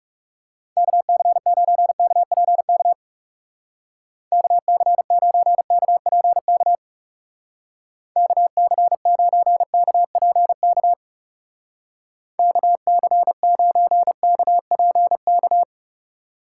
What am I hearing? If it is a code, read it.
Morse code, heard as a keyed tone, about 37 wpm, 700 Hz. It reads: KC9KPK KC9KPK KC9KPK KC9KPK
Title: Speed Racing - USA Call Signs 30wpm